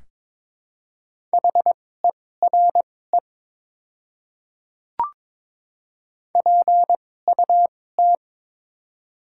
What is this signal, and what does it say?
Morse code, heard as a keyed tone, about 22 wpm, spaced as for 16 wpm, 700 Hz. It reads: HERE PUT
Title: 2x Character Spacing - Top 200 Words - 22wpm